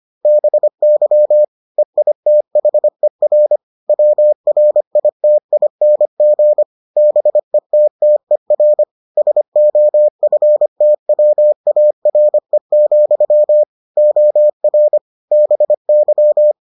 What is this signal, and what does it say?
Morse code, heard as a keyed tone, about 25 wpm, 600 Hz. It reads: BY EITHER WRITING BETTER SOFTWARE, OR BY